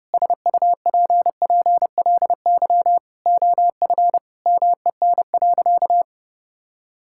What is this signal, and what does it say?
Morse code, heard as a keyed tone, about 30 wpm, 700 Hz. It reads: SUPPLY OF MEN.
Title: Sentences from Top 1000 Words - Review Part 5 - 30wpm